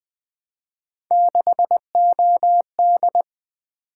Morse code, heard as a keyed tone, about 20 wpm, 700 Hz. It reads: 6OD
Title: Three Letter-Number Combinations - 20wpm